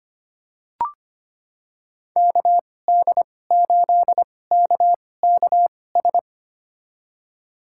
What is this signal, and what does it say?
Morse code, heard as a keyed tone, about 25 wpm, spaced as for 18 wpm, 700 Hz. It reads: KD8KKS
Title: USA Call Signs - 6 Characters - 2x Character Spacing - 25wpm